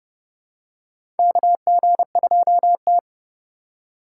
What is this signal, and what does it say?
Morse code, heard as a keyed tone, about 29 wpm, 700 Hz. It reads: KG2T
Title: USA Call Signs - 4 Characters - 8x Character Spacing - 30wpm